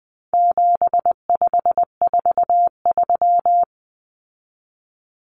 Morse code, heard as a keyed tone, about 20 wpm, 700 Hz. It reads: 7543
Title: Common Cut Numbers - 4-Digit 20wpm